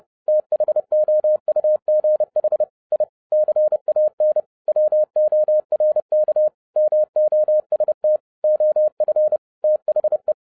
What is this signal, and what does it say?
Morse code, heard as a keyed tone, about 30 wpm, 610 Hz. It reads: THOUGH I CAN WORK MOST OF THE